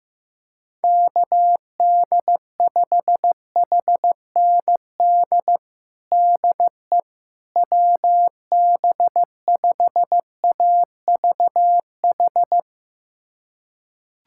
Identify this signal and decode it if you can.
Morse code, heard as a keyed tone, about 15 wpm, 700 Hz. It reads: KD5HND DE WB5AVH
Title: USA Callsign DE Callsign - 6 Characters - 15wpm